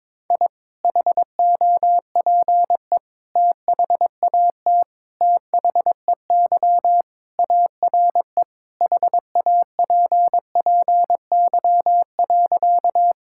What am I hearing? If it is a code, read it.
Morse code, heard as a keyed tone, about 22 wpm, 700 Hz. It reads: I HOPE THAT THEY ARE HAPPY.
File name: practice / 22wpm / Sentences with 3-Word Phrases - 22wpm